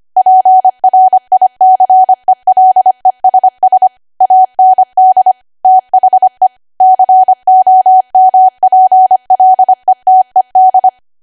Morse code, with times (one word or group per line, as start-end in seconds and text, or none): 0.2-3.9 PRICELESS
4.2-5.3 AND
5.7-6.5 THE
6.8-10.9 COMPLETED